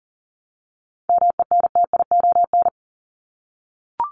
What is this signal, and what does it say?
Morse code, heard as a keyed tone, about 40 wpm, 700 Hz. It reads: MENTION
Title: Top 1501-1750 Words 40wpm